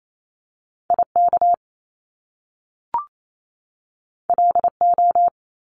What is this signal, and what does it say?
Morse code, heard as a keyed tone, about 28 wpm, 700 Hz. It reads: IK LO
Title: Top Two Letter Combinations in Words 28wpm